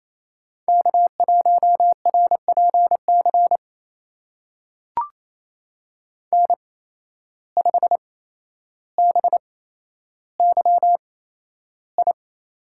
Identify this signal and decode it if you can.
Morse code, heard as a keyed tone, about 28 wpm, spaced as for 12 wpm, 700 Hz. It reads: K1RPC N5BYI
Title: USA Call Signs - 5 Characters - 8x Character Spacing - 28wpm